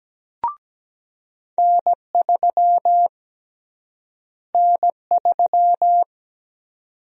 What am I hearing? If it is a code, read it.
Morse code, heard as a keyed tone, about 17 wpm, 700 Hz. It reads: N3 N3
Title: USA Call Sign Prefixes - Repeated 2x - 17wpm